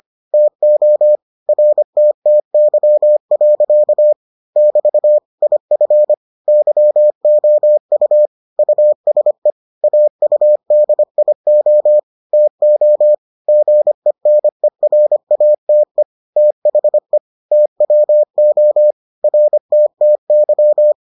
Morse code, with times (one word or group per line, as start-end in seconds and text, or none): 0.3-1.2 TO
1.5-4.1 RTTY.
4.6-5.2 <BT>
5.4-6.1 IF
6.5-8.3 YOU
8.6-9.5 USE
9.8-12.0 AUDIO
12.3-13.1 TO
13.5-16.0 GENERATE
16.4-17.2 THE
17.5-18.9 TWO
19.2-20.9 RTTY